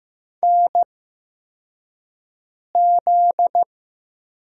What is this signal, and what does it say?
Morse code, heard as a keyed tone, about 15 wpm, spaced as for 4 wpm, 700 Hz. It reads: NZ
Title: USA Call Signs - 4 Characters - 8x Character Spacing - 15wpm